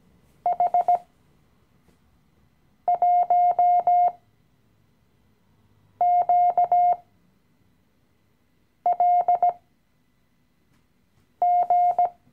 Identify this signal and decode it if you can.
Morse code, heard as a keyed tone, about 17 wpm, spaced as for 4 wpm, 710 Hz. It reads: H1QLG
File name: CW-Online-Kurs / Anfänger / Lektion 015